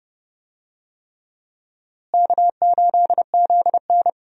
Morse code, heard as a keyed tone, about 30 wpm, 700 Hz. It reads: K8ZN
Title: USA Call Signs - 4 Characters - Repeated 3x - 30wpm